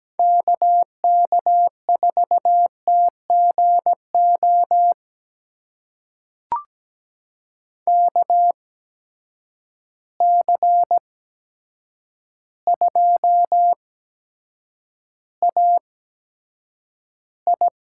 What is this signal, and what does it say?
Morse code, heard as a keyed tone, about 17 wpm, spaced as for 8 wpm, 700 Hz. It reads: KK4TGO KC2AI
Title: USA Call Signs - 6 Characters - 8x Character Spacing - 17wpm